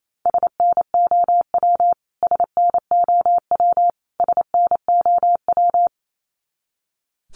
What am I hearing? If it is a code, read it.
Morse code, heard as a keyed tone, about 28 wpm, 700 Hz. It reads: SNOW SNOW SNOW
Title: Top 500 Words Repeated 28wpm